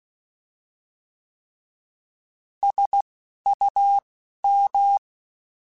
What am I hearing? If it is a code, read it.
Morse code, heard as a keyed tone, about 16 wpm, spaced as for 11 wpm, 800 Hz. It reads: SUM
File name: Alphabetical / Alphabetical mp3s / Ss @10